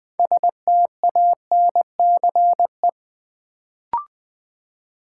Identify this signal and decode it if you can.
Morse code, heard as a keyed tone, about 20 wpm, 700 Hz. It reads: STANCE
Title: Build A Word - 3 Letters At A Time - 6-Letter Words - 20wpm